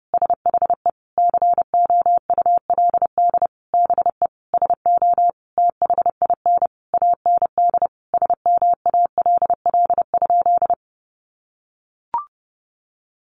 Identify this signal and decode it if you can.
Morse code, heard as a keyed tone, about 30 wpm, 700 Hz. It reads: SHE COULD BE SO THIN AND SMALL?
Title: Sentences from Top 1200 Words 30wpm